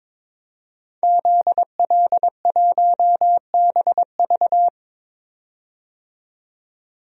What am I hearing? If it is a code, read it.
Morse code, heard as a keyed tone, about 22 wpm, 700 Hz. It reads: ZL1BV
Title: DXCC-100 Call Signs - 5 Characters - 22wpm